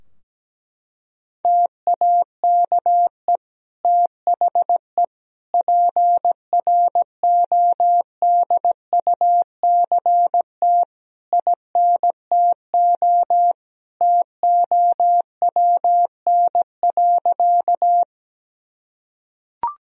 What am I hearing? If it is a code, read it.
Morse code, heard as a keyed tone, about 17 wpm, 700 Hz. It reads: TAKE THE PRODUCT INTO TOWN.
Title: Sentences from Top 1000 Words - Review Part 2 - 17wpm